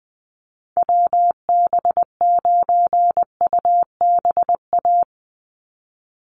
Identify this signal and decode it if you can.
Morse code, heard as a keyed tone, about 20 wpm, 700 Hz. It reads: WB9UBA
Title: USA Call Signs 20wpm